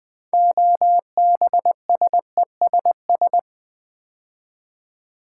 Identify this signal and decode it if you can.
Morse code, heard as a keyed tone, about 20 wpm, 700 Hz. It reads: OBSESS
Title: Build A Word - 3 Letters At A Time - 9-Letter Words - 20wpm